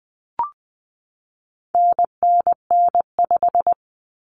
Morse code, heard as a keyed tone, about 20 wpm, 700 Hz. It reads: NNN5
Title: Common Cut Numbers - 4-Digit 20wpm